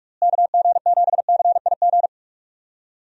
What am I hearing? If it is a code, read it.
Morse code, heard as a keyed tone, about 44 wpm, 700 Hz. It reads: KG7CIG
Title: USA Call Signs - 6 Characters - Repeated 2x - 45wpm